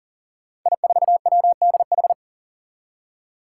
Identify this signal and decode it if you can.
Morse code, heard as a keyed tone, about 40 wpm, 700 Hz. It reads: I4WDH
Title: Five Challenging Letter-Number Combinations 40wpm